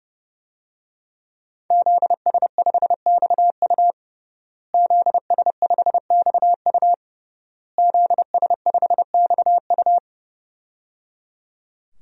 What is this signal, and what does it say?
Morse code, heard as a keyed tone, about 30 wpm, 700 Hz. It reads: ZS5XU ZS5XU ZS5XU
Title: DXCC-100 Call Signs - 5 Characters - Repeated 3x - 30wpm